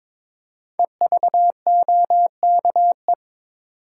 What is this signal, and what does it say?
Morse code, heard as a keyed tone, about 22 wpm, 700 Hz. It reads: EVOKE
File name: 5 Letter Words - First 3 Letters Given - 22wpm